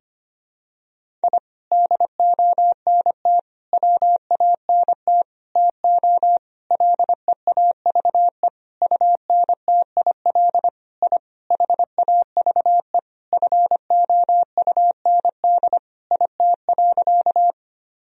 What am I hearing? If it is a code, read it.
Morse code, heard as a keyed tone, about 25 wpm, 700 Hz. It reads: I DONT WANT TO LEAVE UNTIL I HAVE FOUND IT.
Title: Sentences from Top 1000 Words - Review Part 1 - 25wpm